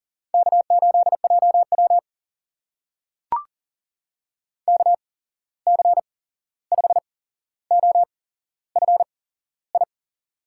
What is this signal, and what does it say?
Morse code, heard as a keyed tone, about 40 wpm, spaced as for 15 wpm, 700 Hz. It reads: K8JW KC5OFI
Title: USA Call Signs - 8x Character Spacing - 40wpm